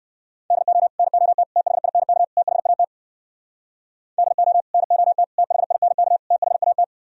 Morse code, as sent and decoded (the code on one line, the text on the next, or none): -.. --- / -. --- - / - .... .. -. -.- / - .... .- - / -.. --- / -. --- - / - .... .. -. -.- / - .... .- -
DO NOT THINK THAT DO NOT THINK THAT